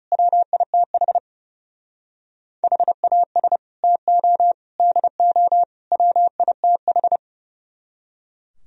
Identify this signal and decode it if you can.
Morse code, heard as a keyed tone, about 36 wpm, 700 Hz. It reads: WITH HAS TO DO WITH
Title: Speed-Racing - Sets of 4 Words 30wpm